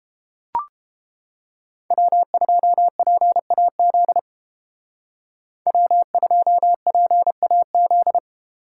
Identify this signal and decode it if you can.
Morse code, heard as a keyed tone, about 31 wpm, 700 Hz. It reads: W2PAZ W2PAZ